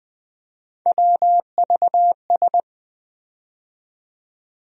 Morse code, heard as a keyed tone, about 20 wpm, 700 Hz. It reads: WVS
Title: USA Call Sign Postfixes - 3 Letters 20wpm